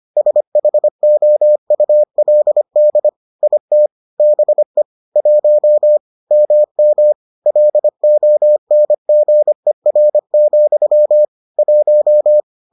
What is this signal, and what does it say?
Morse code, heard as a keyed tone, about 25 wpm, 600 Hz. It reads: SHOULD IT BE 1 MM LONGER, 1